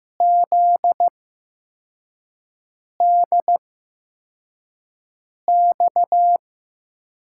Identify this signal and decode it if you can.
Morse code, heard as a keyed tone, about 15 wpm, spaced as for 4 wpm, 700 Hz. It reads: ZDX
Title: USA Call Signs - 8x Character Spacing - 15wpm